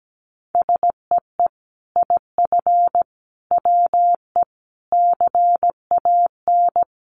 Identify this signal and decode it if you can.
Morse code, heard as a keyed tone, about 17 wpm, 700 Hz. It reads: SEE IF WE CAN
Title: Sets of 5 Words Encore2 17wpm